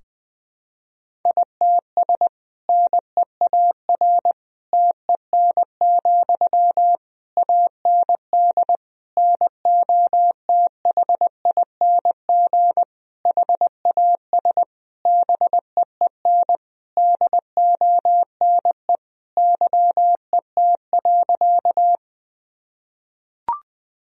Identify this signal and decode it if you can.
Morse code, heard as a keyed tone, about 20 wpm, 700 Hz. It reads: ITS NEAR TEN, AND NOTHING HAS BEEN DONE YET.
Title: Sentences from Top 1000 Words - Review Part 2 - 20wpm